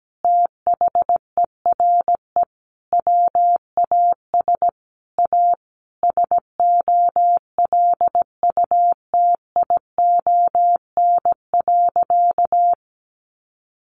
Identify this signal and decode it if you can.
Morse code, heard as a keyed tone, about 17 wpm, 700 Hz. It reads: THERE WAS A SOLUTION.
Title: Sentences from Top 1300 Words 17wpm